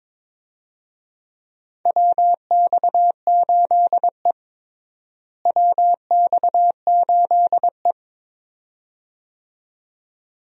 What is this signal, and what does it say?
Morse code, heard as a keyed tone, about 22 wpm, 700 Hz. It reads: WX8E WX8E
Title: USA Call Signs - 4 Characters - Repeated 2x - 22wpm